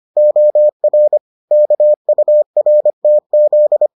Morse code, heard as a keyed tone, about 25 wpm, 600 Hz. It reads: OR KURTZ